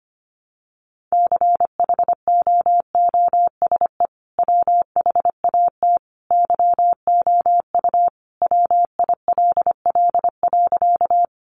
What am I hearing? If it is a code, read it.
Morse code, heard as a keyed tone, about 25 wpm, 700 Hz. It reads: CHOOSE WHAT YOU WILL.
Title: Sentences from Top 700 Words 25wpm